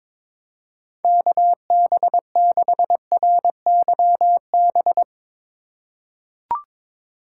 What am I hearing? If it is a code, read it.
Morse code, heard as a keyed tone, about 22 wpm, 700 Hz. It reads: KB6RYB
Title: USA Call Signs - 6 Characters - 4x Character Spacing - 22wpm